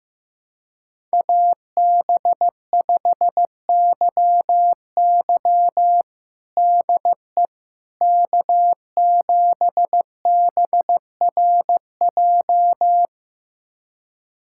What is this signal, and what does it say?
Morse code, heard as a keyed tone, about 15 wpm, 700 Hz. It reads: AB5YY DE K7BRJ
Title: USA Callsign DE Callsign - 5 Characters - 15wpm